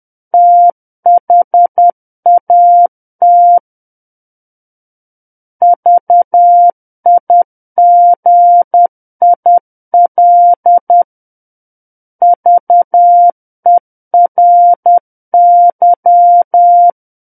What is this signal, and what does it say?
Morse code, heard as a keyed tone, about 10 wpm, 700 Hz. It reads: THAT VIGIL VERY
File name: Book One / 01-Chapter 1 - 10 WPM